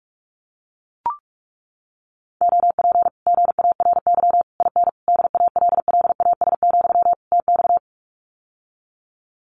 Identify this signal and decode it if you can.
Morse code, heard as a keyed tone, about 44 wpm, 700 Hz. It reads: OP GARY IN DALLAS, TX
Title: Common QSO Phrases Encore 45wpm